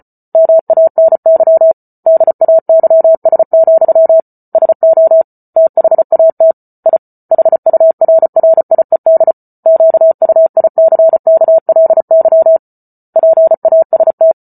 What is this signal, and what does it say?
Morse code, heard as a keyed tone, about 34 wpm, 650 Hz. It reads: MANY DAYS, SO THAT I HURRIED QUICKLY PAST